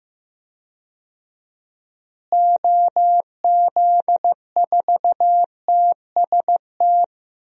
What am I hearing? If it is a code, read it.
Morse code, heard as a keyed tone, about 15 wpm, 700 Hz. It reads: OZ4TST
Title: DXCC-100 Call Signs - 6 Characters - Repeated 3x - 15wpm